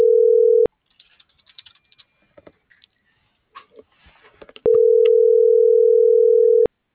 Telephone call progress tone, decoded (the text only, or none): ringback tone